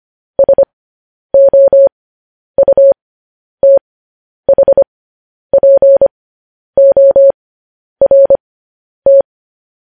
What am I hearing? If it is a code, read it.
Morse code, heard as a keyed tone, about 25 wpm, spaced as for 10 wpm, 550 Hz. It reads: SOUTHPORT
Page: news-headlines-in-morse-code-at-10-wpm